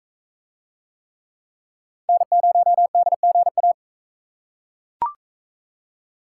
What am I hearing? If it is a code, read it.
Morse code, heard as a keyed tone, about 41 wpm, 700 Hz. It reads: N0DGA